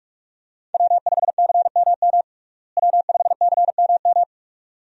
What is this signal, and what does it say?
Morse code, heard as a keyed tone, about 46 wpm, 700 Hz. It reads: W5CMM W5CMM
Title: USA Call Signs - 5 Characters - Repeated 2x - 45wpm